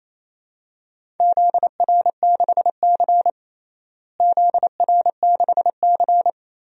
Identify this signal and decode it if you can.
Morse code, heard as a keyed tone, about 28 wpm, 700 Hz. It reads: ZR6C ZR6C
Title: DXCC-100 Call Signs - 4 Characters - Repeated 3x - 28wpm